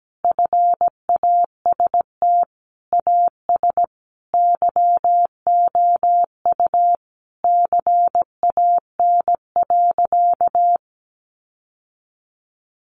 Morse code, heard as a keyed tone, about 17 wpm, 700 Hz. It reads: FAST AS YOU CAN.
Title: Sentences from Top 600 Words 17wpm